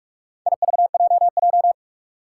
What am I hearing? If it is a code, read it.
Morse code, heard as a keyed tone, about 44 wpm, 700 Hz. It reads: IVJJ
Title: Four Challenging Letter-Number Combinations 45wpm